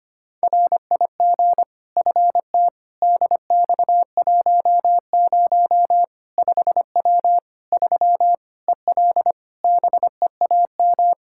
Morse code, heard as a keyed tone, about 25 wpm, 700 Hz. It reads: RIG FT DX10 5W 3 EL BEAM
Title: Practice QSO - Exchange 2 - Info - 25wpm